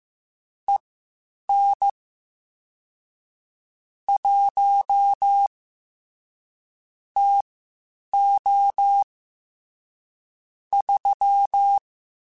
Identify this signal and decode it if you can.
Morse code, heard as a keyed tone, about 15 wpm, spaced as for 8 wpm, 800 Hz. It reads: EN 1 TO 3
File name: Spaced Characters / Spaced Characters-mp3s / Weather5 1 sp @13